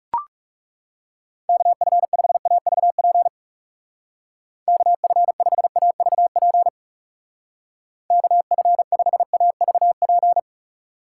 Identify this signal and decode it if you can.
Morse code, heard as a keyed tone, about 37 wpm, 700 Hz. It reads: KF5AVP KF5AVP KF5AVP
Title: Speed Racing - USA 2x3 Call Signs 30wpm